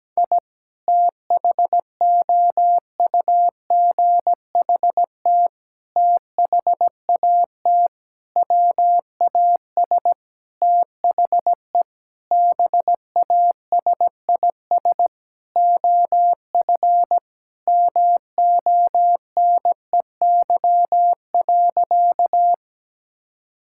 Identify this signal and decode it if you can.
Morse code, heard as a keyed tone, about 17 wpm, 700 Hz. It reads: I THOUGHT THAT WAS THE BASIS OF MONEY.
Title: Sentences from Top 1000 Words - Review Part 5 - 17wpm